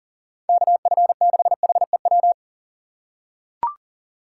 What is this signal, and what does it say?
Morse code, heard as a keyed tone, about 40 wpm, 700 Hz. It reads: KF6HEW
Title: USA Call Signs - 4x Character Spacing - 40wpm